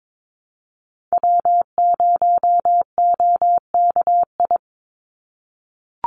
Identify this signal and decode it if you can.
Morse code, heard as a keyed tone, about 22 wpm, 700 Hz. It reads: W0OKI
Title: USA Call Signs 22wpm